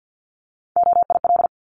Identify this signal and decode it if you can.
Morse code, heard as a keyed tone, about 50 wpm, 700 Hz. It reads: OIL
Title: Top 501-750 Words 50wpm